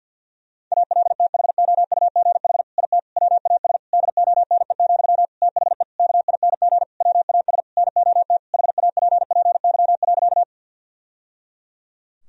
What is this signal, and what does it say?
Morse code, heard as a keyed tone, about 46 wpm, 700 Hz. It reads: ALTHOUGH IT WAS DONE, THE KING WAS NOT HAPPY.
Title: Sentences from Top 1000 Words - Review Part 3 - 50wpm